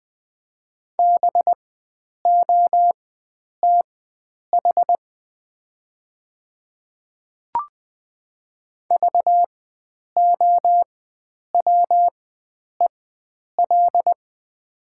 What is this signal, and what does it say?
Morse code, heard as a keyed tone, about 20 wpm, spaced as for 9 wpm, 700 Hz. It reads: BOTH VOWEL
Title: 4x Character Spacing - Top 500 Words - 20wpm